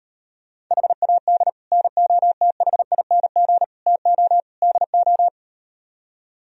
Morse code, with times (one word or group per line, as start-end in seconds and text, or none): 0.7-1.5 HAD
1.7-3.6 NOTHING
3.9-4.4 TO
4.6-5.3 DO